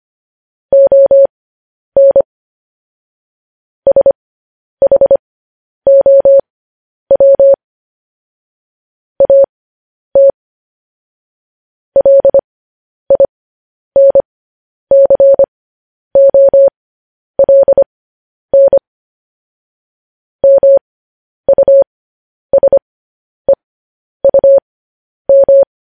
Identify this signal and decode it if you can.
Morse code, heard as a keyed tone, about 25 wpm, spaced as for 10 wpm, 550 Hz. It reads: ON SHOW AT LINCOLN MUSEUM
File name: podcast / News headlines-1x-25x10